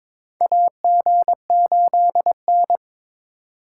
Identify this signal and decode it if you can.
Morse code, heard as a keyed tone, about 22 wpm, 700 Hz. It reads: AG8N
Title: USA Call Signs - 4x Character Spacing - 22wpm